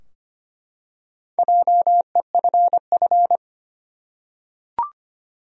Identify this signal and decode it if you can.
Morse code, heard as a keyed tone, about 25 wpm, 700 Hz. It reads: JEFF
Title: USA Common Names - Max 5 Letters - 25wpm